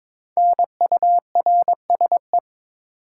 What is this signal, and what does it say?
Morse code, heard as a keyed tone, about 22 wpm, 700 Hz. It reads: NURSE
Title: Five Letter Words - 22wpm